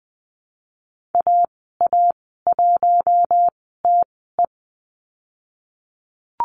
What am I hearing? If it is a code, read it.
Morse code, heard as a keyed tone, about 20 wpm, 700 Hz. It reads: A A 1 T E
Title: 055 Lesson 10g - Letter r - Call Signs 20wpm